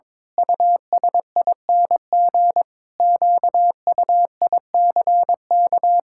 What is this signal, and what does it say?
Morse code, heard as a keyed tone, about 22 wpm, 700 Hz. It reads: USING QUICK
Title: Pangram Challenge v2 - 22wpm